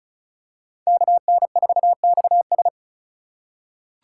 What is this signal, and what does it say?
Morse code, heard as a keyed tone, about 36 wpm, 700 Hz. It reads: KN4XS
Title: USA Call Signs - 5 Characters - 3x Character Spacing - 35wpm